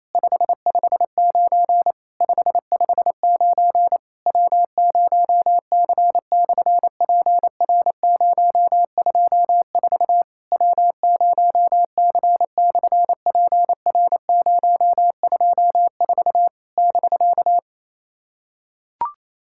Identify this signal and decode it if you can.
Morse code, heard as a keyed tone, about 28 wpm, 700 Hz. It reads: 559 559 W0C/PR024 W0C/PR024 <BK>
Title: SOTA QSO Practice 28wpm